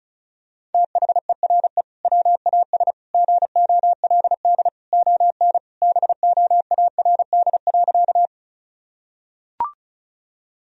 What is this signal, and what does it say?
Morse code, heard as a keyed tone, about 36 wpm, 700 Hz. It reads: THERE WAS GOLD ON BOARD.
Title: Sentences from Top 1000 Words - Review Part 4 - 35wpm